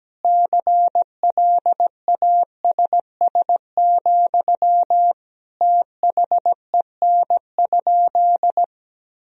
Morse code, heard as a keyed tone, about 17 wpm, 700 Hz. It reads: CLASS, THEN?
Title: Sentences from Top 1000 Words - Review Part 4 - 17wpm